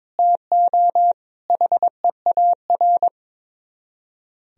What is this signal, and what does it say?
Morse code, heard as a keyed tone, about 22 wpm, 700 Hz. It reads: TO HEAR
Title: Speed-Racing - Sets of 5 Words 22wpm